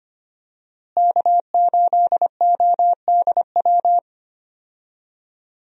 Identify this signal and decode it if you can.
Morse code, heard as a keyed tone, about 25 wpm, 700 Hz. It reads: K8ODW
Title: USA Call Signs - 5 Characters - Repeated 3x - 25wpm